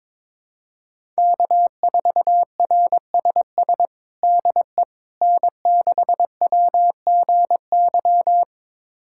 Morse code, heard as a keyed tone, about 22 wpm, 700 Hz. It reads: K4RSS DE N6WGY
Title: USA Callsign DE Callsign 22wpm